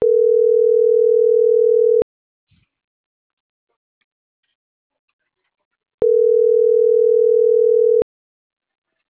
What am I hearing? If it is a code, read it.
Ringback tone (North America), 440 + 480 Hz.